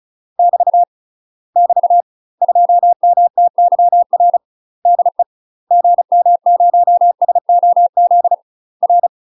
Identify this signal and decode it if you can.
Morse code, heard as a keyed tone, about 34 wpm, 700 Hz. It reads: <BT> <BT> 2MTYR DE GM0SOZ R
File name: UK MASTER QSOs TRAINING / 35WPM / UK MASTER QSOs 0024